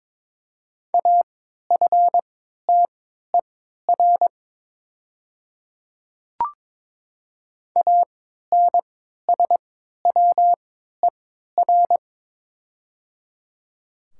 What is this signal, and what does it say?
Morse code, heard as a keyed tone, about 22 wpm, spaced as for 12 wpm, 700 Hz. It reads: AFTER ANSWER
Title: 3x Character Spacing - Top 500 Words - 22wpm